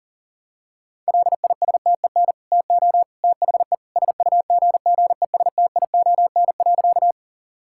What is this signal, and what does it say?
Morse code, heard as a keyed tone, about 40 wpm, 700 Hz. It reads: LISTEN TO THE SUGGESTION.
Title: Sentences from Top 1600 Words 40wpm